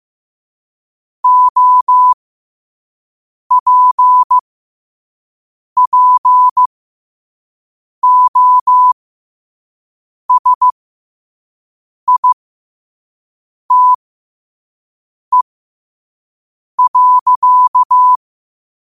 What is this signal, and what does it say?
Morse code, heard as a keyed tone, about 15 wpm, spaced as for 5 wpm, 1000 Hz. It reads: OPPOSITE.
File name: qotdfiles / 5wpm